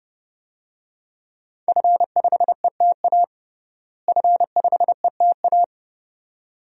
Morse code, heard as a keyed tone, about 30 wpm, 700 Hz. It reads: F5ETA F5ETA
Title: DXCC-100 Call Signs - 5 Characters - Repeated 2x - 30wpm